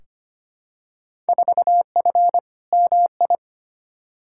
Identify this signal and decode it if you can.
Morse code, heard as a keyed tone, about 25 wpm, 700 Hz. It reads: 4F MI
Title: Field Day Exchanges - 25wpm